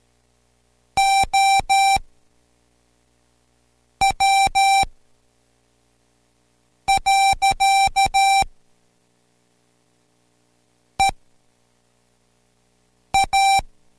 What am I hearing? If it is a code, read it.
Morse code, heard as a keyed tone, about 13 wpm, spaced as for 4 wpm, 780 Hz. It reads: OW. EA